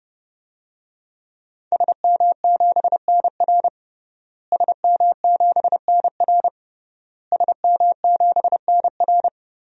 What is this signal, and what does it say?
Morse code, heard as a keyed tone, about 30 wpm, 700 Hz. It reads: SM7NR SM7NR SM7NR
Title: DXCC-100 Call Signs - 5 Characters - Repeated 3x - 30wpm